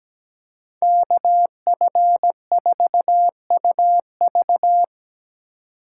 Morse code, heard as a keyed tone, about 20 wpm, 700 Hz. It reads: KF4UV
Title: Speed Racing - USA 2x2 Call Signs 17wpm